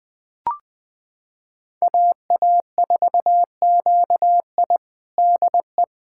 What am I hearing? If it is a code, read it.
Morse code, heard as a keyed tone, about 21 wpm, 700 Hz. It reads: AA4QI DE
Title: USA Callsign DE Callsign - 5 Characters - 20wpm